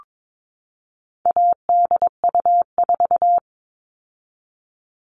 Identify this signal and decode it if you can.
Morse code, heard as a keyed tone, about 22 wpm, 700 Hz. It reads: ADU4
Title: All Cut Numbers - 4-Digit 22wpm